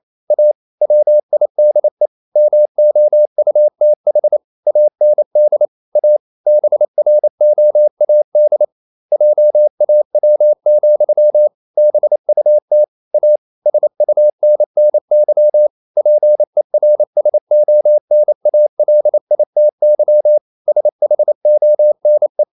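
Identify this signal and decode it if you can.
Morse code, heard as a keyed tone, about 28 wpm, 600 Hz. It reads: A WIDE MOUTH AND A BROAD JAW, BUT A SUNNY PERSONALITY SHONE
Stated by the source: Morse code practice at 28 wpm, text of Ken Follett's novel Never in English, from CW med Gnister.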